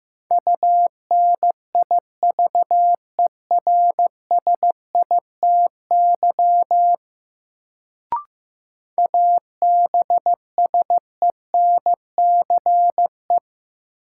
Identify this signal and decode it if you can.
Morse code, heard as a keyed tone, about 15 wpm, 700 Hz. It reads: UNIVERSITY ABSENCE
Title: Sets of 4 Words - New Word Drill - 15wpm